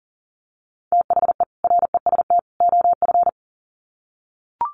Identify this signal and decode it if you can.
Morse code, heard as a keyed tone, about 40 wpm, 700 Hz. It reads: THE REST OF
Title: Sets of 3 Words 40wpm